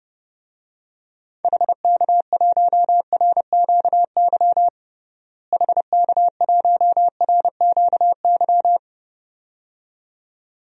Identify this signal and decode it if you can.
Morse code, heard as a keyed tone, about 31 wpm, 700 Hz. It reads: HK1RQY HK1RQY
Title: DXCC-100 Call Signs - 6 Characters - Repeated 2x - 30wpm